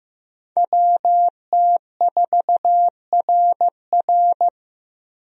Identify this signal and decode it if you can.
Morse code, heard as a keyed tone, about 15 wpm, 700 Hz. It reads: WT4RR
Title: USA Call Signs - 5 Characters - Repeated 2x - 15wpm